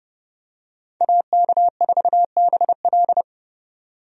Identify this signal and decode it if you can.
Morse code, heard as a keyed tone, about 30 wpm, 700 Hz. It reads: AK4BL
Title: USA Call Signs - 5 Characters - 30wpm